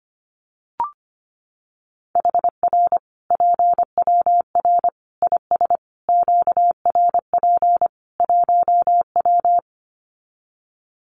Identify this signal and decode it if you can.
Morse code, heard as a keyed tone, about 25 wpm, 700 Hz. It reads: HR PWR IS QRP 1W
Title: Common QSO Phrases 25wpm